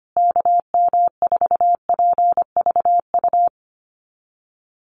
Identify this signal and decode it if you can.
Morse code, heard as a keyed tone, about 25 wpm, 700 Hz. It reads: KM4PVU
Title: USA Call Signs Repeated 3x 25wpm